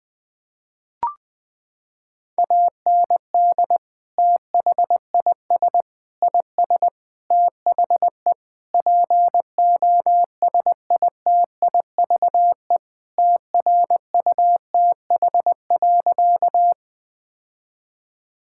Morse code, heard as a keyed tone, about 20 wpm, 700 Hz. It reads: AND THIS IS THE POSITIVE TRUTH.
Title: Sentences from Top 1000 Words - Review Part 5 - 20wpm